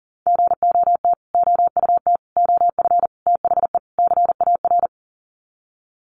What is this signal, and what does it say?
Morse code, heard as a keyed tone, about 40 wpm, 700 Hz. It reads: GOT OUT OF THE CAR
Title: Sets of 5 Words Encore 40wpm